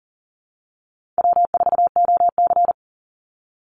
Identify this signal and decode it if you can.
Morse code, heard as a keyed tone, about 40 wpm, 700 Hz. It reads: W4OC
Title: Easy Call Signs - 40wpm